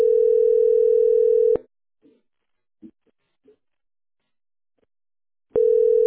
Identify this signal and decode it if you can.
Ringback tone (North America), 440 + 480 Hz.